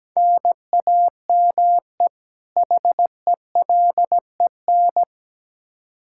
Morse code, heard as a keyed tone, about 17 wpm, 700 Hz. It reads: NAME HELEN
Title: Practice QSO - Exchange 1 - Beginning - 17wpm